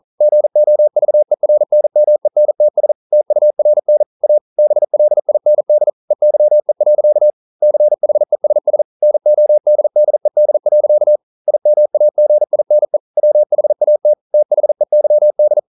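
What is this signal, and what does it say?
Morse code, heard as a keyed tone, about 42 wpm, 600 Hz. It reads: GOVERNMENTS TURN A BLIND EYE. CHESS NODDED. IMAGINE WHAT THEYD
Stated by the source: Morse code practice at 41 wpm, from CW med Gnister.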